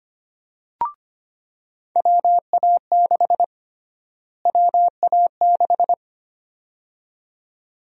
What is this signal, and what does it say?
Morse code, heard as a keyed tone, about 25 wpm, 700 Hz. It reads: WA6 WA6
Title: USA Call Sign Prefixes - Repeated 2x - 25wpm